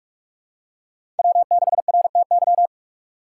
Morse code, heard as a keyed tone, about 47 wpm, 700 Hz. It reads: W6RTY
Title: USA Call Signs - 5 Characters - 6x Character Spacing - 45wpm